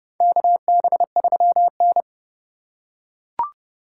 Morse code, heard as a keyed tone, about 30 wpm, 700 Hz. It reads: KB3N
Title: USA Call Signs - 4 Characters - 4x Character Spacing - 30wpm